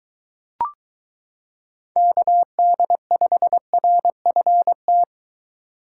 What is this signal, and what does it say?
Morse code, heard as a keyed tone, about 23 wpm, 700 Hz. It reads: KD5RFT